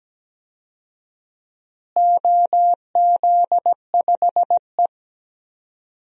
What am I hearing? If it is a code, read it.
Morse code, heard as a keyed tone, about 17 wpm, 700 Hz. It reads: OZ5E
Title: DXCC-100 Call Signs - 4 Characters - Repeated 3x - 17wpm